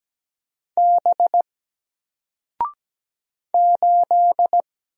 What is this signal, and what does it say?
Morse code, heard as a keyed tone, about 17 wpm, 700 Hz. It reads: B 8
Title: Single Character (Letter:Number:Punctuation:Prosign) Alt 17wpm